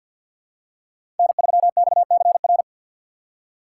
Morse code, heard as a keyed tone, about 46 wpm, 700 Hz. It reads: N3XCR
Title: USA Call Signs - 5 Characters - 4x Character Spacing - 50wpm